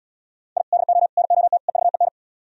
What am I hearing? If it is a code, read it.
Morse code, heard as a keyed tone, about 77 wpm, 700 Hz. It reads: I DO NOT EVEN